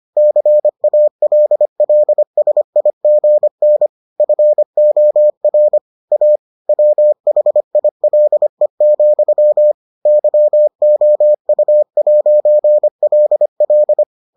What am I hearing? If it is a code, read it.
Morse code, heard as a keyed tone, about 25 wpm, 600 Hz. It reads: CALLSIGN FOR A WHILE, YOU'LL